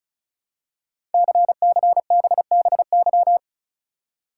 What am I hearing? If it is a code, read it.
Morse code, heard as a keyed tone, about 34 wpm, 700 Hz. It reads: CCBBY
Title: Code Groups - 5 Letters - 35wpm